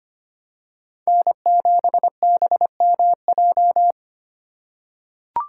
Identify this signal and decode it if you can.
Morse code, heard as a keyed tone, about 25 wpm, 700 Hz. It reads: N7BMJ
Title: USA Call Signs - 5 Characters - 4x Character Spacing - 25wpm